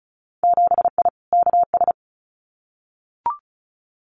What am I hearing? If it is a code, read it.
Morse code, heard as a keyed tone, about 36 wpm, 700 Hz. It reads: 7I KS
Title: Winter Field Day Exchanges Encore 35wpm